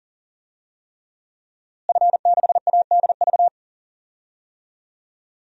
Morse code, heard as a keyed tone, about 40 wpm, 700 Hz. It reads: F6ADV
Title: DXCC-100 Call Signs - 5 Characters - Repeated 3x - 40wpm